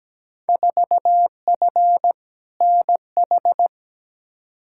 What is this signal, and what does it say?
Morse code, heard as a keyed tone, about 17 wpm, 700 Hz. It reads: 4F NH
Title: Field Day Exchanges Encore - 17wpm